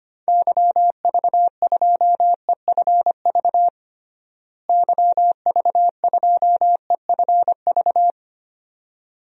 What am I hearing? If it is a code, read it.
Morse code, heard as a keyed tone, about 25 wpm, 700 Hz. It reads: YV2EFV YV2EFV
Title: DXCC-100 Call Signs - 6 Characters - Repeated 2x - 25wpm